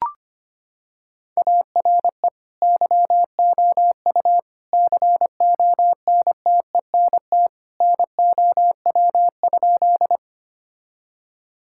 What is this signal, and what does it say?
Morse code, heard as a keyed tone, about 25 wpm, 700 Hz. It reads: ARE YOU CONTENT NOW?
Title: Sentences from Top 1000 Words - Review Part 6 - 25wpm